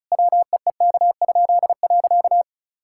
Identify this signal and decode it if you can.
Morse code, heard as a keyed tone, about 36 wpm, 700 Hz. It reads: WEEK?.
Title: Pangram Challenge v2 - 35wpm